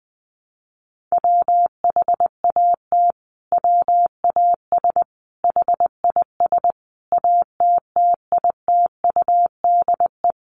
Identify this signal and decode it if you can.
Morse code, heard as a keyed tone, about 20 wpm, 700 Hz. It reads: WHAT WAS HIS ATTITUDE
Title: Sentences from Top 1300 Words 20wpm